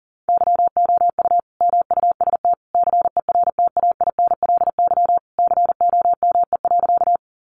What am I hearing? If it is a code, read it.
Morse code, heard as a keyed tone, about 40 wpm, 700 Hz. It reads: YOU MUST CERTAINLY COME.
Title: Sentences from Top 900 Words 40wpm